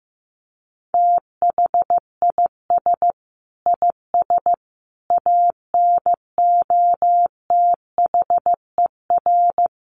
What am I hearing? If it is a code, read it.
Morse code, heard as a keyed tone, about 15 wpm, 700 Hz. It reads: THIS IS ANOTHER
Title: Sentences from Top 1400 Words 15wpm